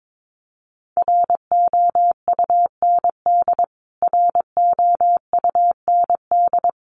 Morse code, heard as a keyed tone, about 22 wpm, 700 Hz. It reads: ROUND ROUND
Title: Top 200 Words Repeated 22wpm